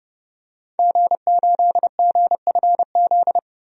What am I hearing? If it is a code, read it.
Morse code, heard as a keyed tone, about 30 wpm, 700 Hz. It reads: G8GFZ
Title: DXCC-100 Call Signs - 5 Characters - 30wpm